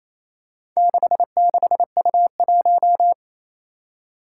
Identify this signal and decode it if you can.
Morse code, heard as a keyed tone, about 28 wpm, 700 Hz. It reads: 66U1
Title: Four Challenging Letter-Number Combinations 28wpm